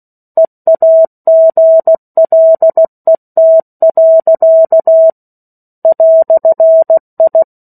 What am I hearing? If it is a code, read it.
Morse code, heard as a keyed tone, about 16 wpm, 650 Hz. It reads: EAGLET. "I
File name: 03-Chapter-03 16 WPM